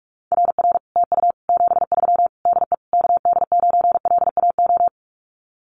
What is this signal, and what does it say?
Morse code, heard as a keyed tone, about 44 wpm, 700 Hz. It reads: RR TU 73 DE KD9LAO
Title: Common QSO Phrases Encore 45wpm